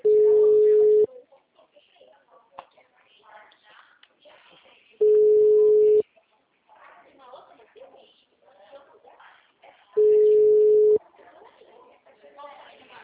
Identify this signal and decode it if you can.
Ringback tone (Europe, 425 Hz), with speech.
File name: out- -2021